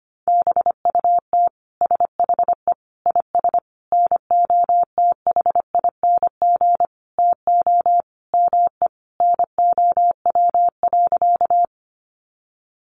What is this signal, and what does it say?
Morse code, heard as a keyed tone, about 25 wpm, 700 Hz. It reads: BUT SHE IS NOTHING TO ME NOW.
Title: Sentences from Top 500 Words 25wpm